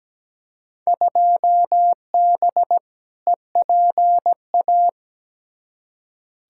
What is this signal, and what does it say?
Morse code, heard as a keyed tone, about 17 wpm, 700 Hz. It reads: 2B EPA
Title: Field Day Exchanges Encore - 17wpm